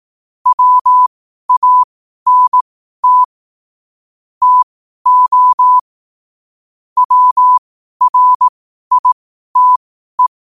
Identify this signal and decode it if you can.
Morse code, heard as a keyed tone, about 18 wpm, spaced as for 12 wpm, 1000 Hz. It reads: WANT TO WRITE
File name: qotdfiles / 12wpm